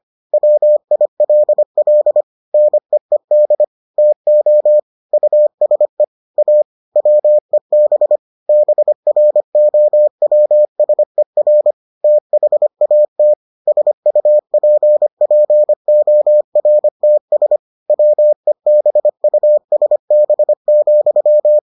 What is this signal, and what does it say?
Morse code, heard as a keyed tone, about 25 wpm, 600 Hz. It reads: WILL NEED TO USE A WEB BROWSER THAT SUPPORTS WEBUSB,